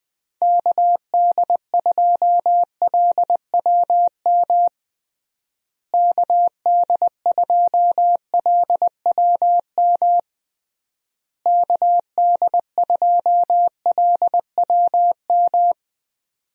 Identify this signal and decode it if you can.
Morse code, heard as a keyed tone, about 20 wpm, 700 Hz. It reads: KD2LWM KD2LWM KD2LWM
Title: USA Call Signs - 6 Characters - Repeated 3x - 20wpm